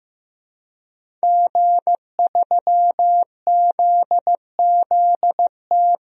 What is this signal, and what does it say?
Morse code, heard as a keyed tone, about 15 wpm, 700 Hz. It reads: G3ZZT
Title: DXCC-100 Call Signs - 5 Characters - 15wpm